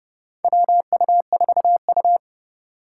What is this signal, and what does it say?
Morse code, heard as a keyed tone, about 30 wpm, 700 Hz. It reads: WU4U
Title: USA Call Signs - 3x Character Spacing - 30wpm